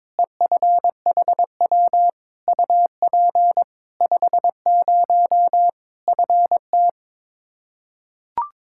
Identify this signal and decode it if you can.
Morse code, heard as a keyed tone, about 22 wpm, 700 Hz. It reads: EFHW UP 50 FT
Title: Practice QSO - Exchange 2 - Info - 22wpm